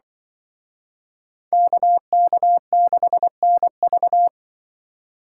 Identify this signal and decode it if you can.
Morse code, heard as a keyed tone, about 24 wpm, 700 Hz. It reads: KK6NV
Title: Speed Racing - USA 2x2 Call Signs 20wpm